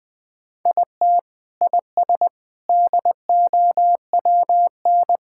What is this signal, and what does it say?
Morse code, heard as a keyed tone, about 20 wpm, 700 Hz. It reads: IT IS DOWN
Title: Sentences with 3-Word Phrases - 20wpm